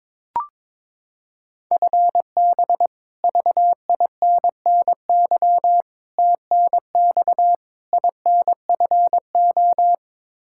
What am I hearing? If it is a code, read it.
Morse code, heard as a keyed tone, about 22 wpm, 700 Hz. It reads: FB VINNY TNX INFO
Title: Practice QSO - Exchange 3 - More Info - 22wpm